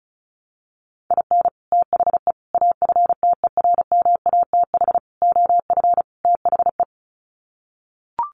Morse code, heard as a keyed tone, about 34 wpm, 700 Hz. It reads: IN THE AFTERMATH OF THE
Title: Sets of 5 Words 35wpm